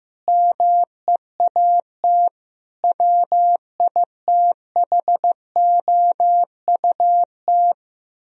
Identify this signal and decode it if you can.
Morse code, heard as a keyed tone, about 15 wpm, 700 Hz. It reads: MEAT WITHOUT
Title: Sentences from Top 1000 Words - Review Part 3 - 15wpm